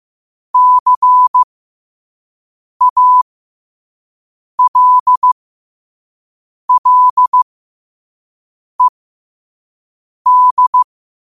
Morse code, heard as a keyed tone, about 15 wpm, spaced as for 5 wpm, 1000 Hz. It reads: CALLED
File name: qotdfiles / 5wpm